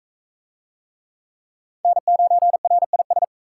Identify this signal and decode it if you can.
Morse code, heard as a keyed tone, about 40 wpm, 700 Hz. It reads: N9RIS